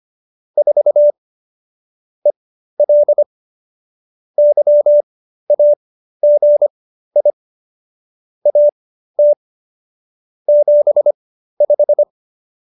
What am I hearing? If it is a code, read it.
Morse code, heard as a keyed tone, about 25 wpm, spaced as for 13 wpm, 590 Hz. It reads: 4 EL YAGI AT 75